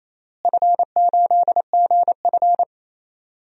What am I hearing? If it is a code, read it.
Morse code, heard as a keyed tone, about 28 wpm, 700 Hz. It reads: F8GF
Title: DXCC-100 Call Signs - 4 Characters - Repeated 3x - 28wpm